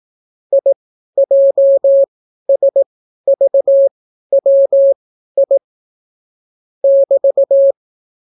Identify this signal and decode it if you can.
Morse code, heard as a keyed tone, about 18 wpm, spaced as for 12 wpm, 550 Hz. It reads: IJSVWI <BT>